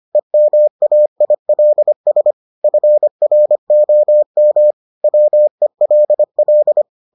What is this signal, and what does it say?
Morse code, heard as a keyed tone, about 25 wpm, 600 Hz. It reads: EMAILS FROM WELL